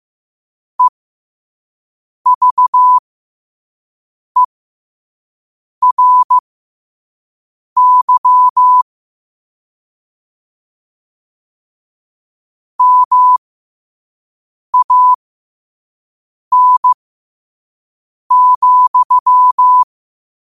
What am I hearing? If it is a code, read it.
Morse code, heard as a keyed tone, about 15 wpm, spaced as for 5 wpm, 1000 Hz. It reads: EVERY MAN,
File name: qotdfiles / 5wpm